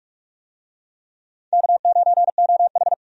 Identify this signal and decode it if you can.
Morse code, heard as a keyed tone, about 46 wpm, 700 Hz. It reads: K9OH
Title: USA Call Signs - 4 Characters - 45wpm